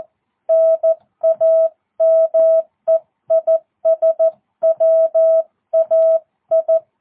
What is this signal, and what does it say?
Morse code, heard as a keyed tone, about 14 wpm, 650 Hz. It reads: NAME IS WAI
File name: jh8cbh / 2022cw